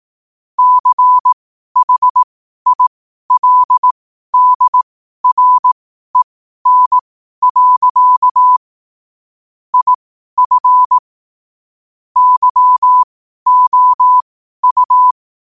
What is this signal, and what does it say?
Morse code, heard as a keyed tone, about 18 wpm, spaced as for 12 wpm, 1000 Hz. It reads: CHILDREN. IF YOU
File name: qotdfiles / 12wpm